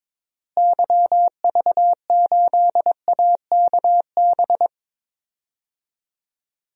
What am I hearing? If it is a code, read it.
Morse code, heard as a keyed tone, about 22 wpm, 700 Hz. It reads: YV8AKB
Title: DXCC-100 Call Signs - 6 Characters - Repeated 3x - 22wpm